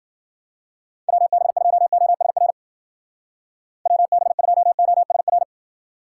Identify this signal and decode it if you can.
Morse code, heard as a keyed tone, about 56 wpm, 700 Hz. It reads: WB2OSR WB2OSR